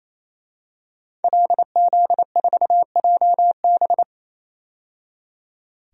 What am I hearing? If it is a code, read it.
Morse code, heard as a keyed tone, about 27 wpm, 700 Hz. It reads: LZ4JB
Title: DXCC-100 Call Signs - 5 Characters - 28wpm